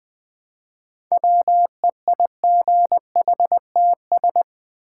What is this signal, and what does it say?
Morse code, heard as a keyed tone, about 20 wpm, 700 Hz. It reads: WEIGHTS
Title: Easy Words with Hints - 20wpm